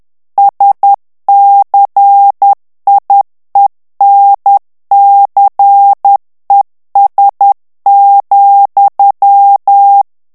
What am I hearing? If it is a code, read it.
Morse code, heard as a keyed tone, about 10 wpm, 800 Hz. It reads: SCIENCES,
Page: Français - texte en clair 1 - vitesse : 10 mots minute